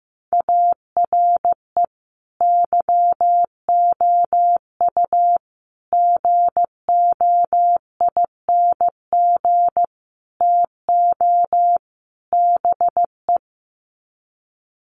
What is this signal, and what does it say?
Morse code, heard as a keyed tone, about 15 wpm, 700 Hz. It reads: ARE YOU GOING TO BE
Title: Sets of 5 Words Encore 15wpm